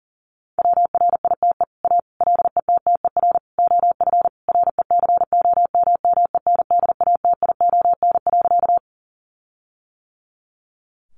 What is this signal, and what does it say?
Morse code, heard as a keyed tone, about 40 wpm, 700 Hz. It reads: WRITE A LETTER OF RECOMMENDATION.
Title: Sentences from Top 1400 Words 40wpm